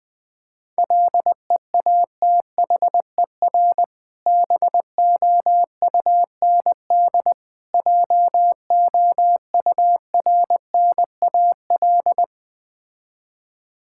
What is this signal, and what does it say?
Morse code, heard as a keyed tone, about 20 wpm, 700 Hz. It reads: LEATHER BOUND JOURNAL
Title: Difficult Words with Hints - 20wpm